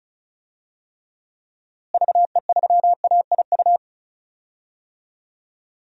Morse code, heard as a keyed tone, about 34 wpm, 700 Hz. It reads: VE3AIU
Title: DXCC-100 Call Signs - 6 Characters - Repeated 2x - 35wpm